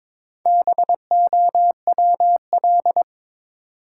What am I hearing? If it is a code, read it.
Morse code, heard as a keyed tone, about 21 wpm, 700 Hz. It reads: BOWL
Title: Sentences from Top 1200 Words - New Word Drill - 22wpm